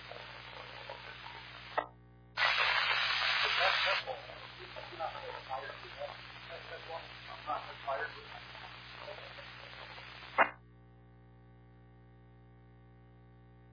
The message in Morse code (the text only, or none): none